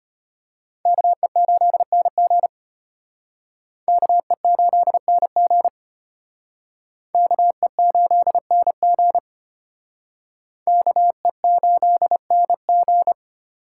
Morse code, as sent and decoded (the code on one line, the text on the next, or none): -.- . ---.. -. --. / -.- . ---.. -. --. / -.- . ---.. -. --. / -.- . ---.. -. --.
KE8NG KE8NG KE8NG KE8NG